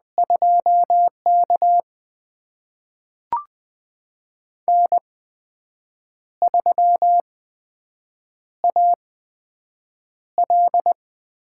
Morse code, heard as a keyed tone, about 20 wpm, spaced as for 7 wpm, 700 Hz. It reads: 2K N3AL
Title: USA Call Signs - 4 Characters - 8x Character Spacing - 20wpm